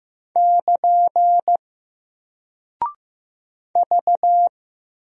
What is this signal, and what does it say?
Morse code, heard as a keyed tone, about 15 wpm, 700 Hz. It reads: <KN> V
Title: Single Character (Letter:Number:Punctuation:Prosign) Alt 15wpm